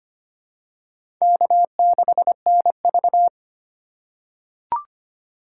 Morse code, heard as a keyed tone, about 25 wpm, 700 Hz. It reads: K6NV
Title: USA Call Signs - 4 Characters - 4x Character Spacing - 25wpm